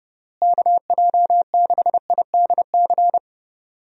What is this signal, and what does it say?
Morse code, heard as a keyed tone, about 30 wpm, 700 Hz. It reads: KJ6IDC
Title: USA Call Signs - 6 Characters - Repeated 2x - 30wpm